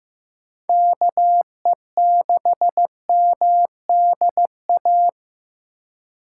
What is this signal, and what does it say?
Morse code, heard as a keyed tone, about 15 wpm, 700 Hz. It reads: KE6MDA
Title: USA Call Signs - 2x Character Spacing - 15wpm